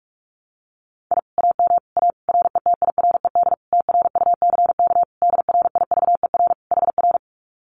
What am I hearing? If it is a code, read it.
Morse code, heard as a keyed tone, about 44 wpm, 700 Hz. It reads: I AM A RETIRED TRUCK DRIVER HR
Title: Common QSO Phrases Encore 45wpm